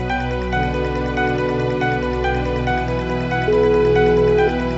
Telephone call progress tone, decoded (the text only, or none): ringback tone